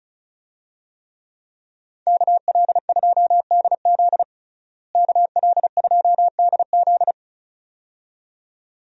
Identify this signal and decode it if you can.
Morse code, heard as a keyed tone, about 34 wpm, 700 Hz. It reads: KL2DZ KL2DZ
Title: USA Call Signs - 5 Characters - Repeated 2x - 35wpm